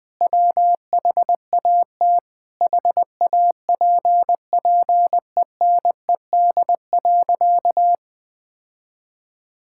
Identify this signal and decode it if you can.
Morse code, heard as a keyed tone, about 20 wpm, 700 Hz. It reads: WHAT HAPPENED.
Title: Sentences with 5-Word Phrases - 20wpm